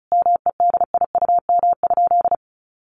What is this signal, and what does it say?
Morse code, heard as a keyed tone, about 34 wpm, 700 Hz. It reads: MEDIUM?
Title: Sentences from Top 900 Words 35wpm